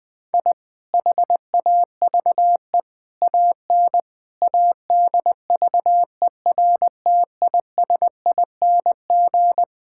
Morse code, heard as a keyed tone, about 20 wpm, 700 Hz. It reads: I HAVE AN ADVERTISING
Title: Sentences from Top 1000 Words - Review Part 6 - 20wpm